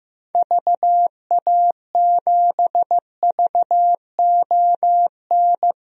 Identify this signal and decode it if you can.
Morse code, heard as a keyed tone, about 15 wpm, 690 Hz. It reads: VA7VON
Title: DXCC-100 Call Signs - 6 Characters - 15wpm